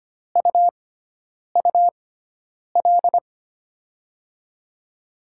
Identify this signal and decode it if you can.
Morse code, heard as a keyed tone, about 25 wpm, spaced as for 9 wpm, 700 Hz. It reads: UUL
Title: USA Call Signs - 6x Character Spacing - 25wpm